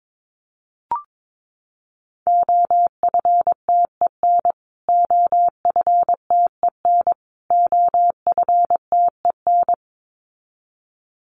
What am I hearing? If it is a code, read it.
Morse code, heard as a keyed tone, about 22 wpm, 700 Hz. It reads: OFTEN OFTEN OFTEN
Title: Top 300 Words Repeated 22wpm